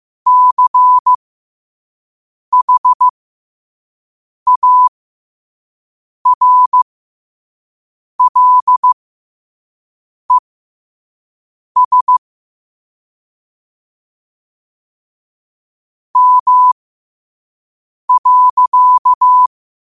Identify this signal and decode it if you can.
Morse code, heard as a keyed tone, about 15 wpm, spaced as for 5 wpm, 1000 Hz. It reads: CHARLES M.
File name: qotdfiles / 5wpm